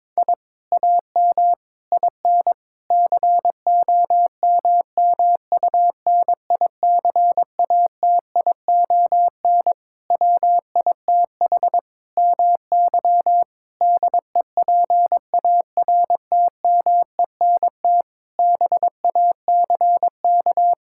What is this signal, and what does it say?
Morse code, heard as a keyed tone, about 22 wpm, 700 Hz. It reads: I AM IN COMMUNICATION WITH MY DEPARTMENT BACK
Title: Sentences from Top 1000 Words - Review Part 6 - 22wpm